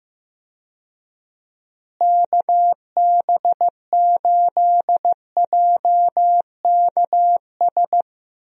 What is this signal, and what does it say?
Morse code, heard as a keyed tone, about 15 wpm, 700 Hz. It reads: KB8JKS
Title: USA Call Signs - 6 Characters - Repeated 2x - 15wpm